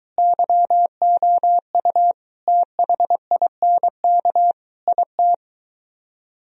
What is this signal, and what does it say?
Morse code, heard as a keyed tone, about 22 wpm, 700 Hz. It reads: YOU THINK IT